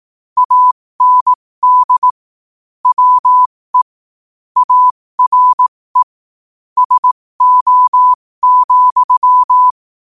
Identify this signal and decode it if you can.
Morse code, heard as a keyed tone, about 18 wpm, spaced as for 15 wpm, 1000 Hz. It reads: AND WE ARE SO,
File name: qotdfiles / 15wpm